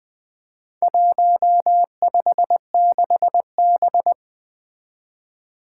Morse code, heard as a keyed tone, about 20 wpm, 700 Hz. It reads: 156B
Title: Four Challenging Letter-Number Combinations 20wpm